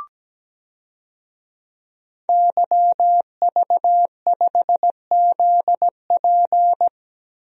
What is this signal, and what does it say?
Morse code, heard as a keyed tone, about 17 wpm, 700 Hz. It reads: YV5ZP
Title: DXCC-100 Call Signs - 5 Characters - 17wpm